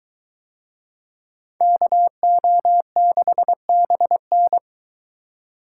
Morse code, heard as a keyed tone, about 21 wpm, 700 Hz. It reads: KO6BN